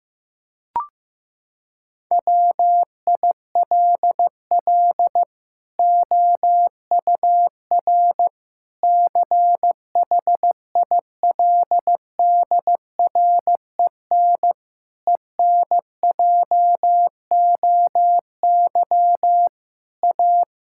Morse code, heard as a keyed tone, about 15 wpm, 700 Hz. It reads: WILL OUR CHILDREN ENJOY A
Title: Sentences from Top 1000 Words - Review Part 5 - 15wpm